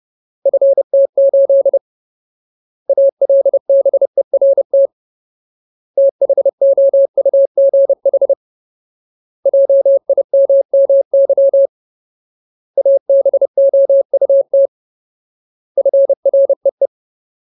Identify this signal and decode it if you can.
Morse code, heard as a keyed tone, about 30 wpm, 550 Hz. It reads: FT8 ALBERT THOUGH JIMMY ABOUT FREE
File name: Ultimate Word List / 30WPM Ultimate